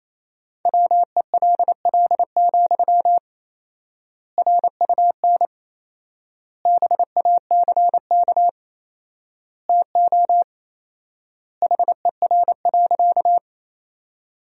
Morse code, heard as a keyed tone, about 28 wpm, 700 Hz. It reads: WELL, RUN BACK TO HER.
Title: 4x Word-Spacing - Sentences from Top 300 Words - 28wpm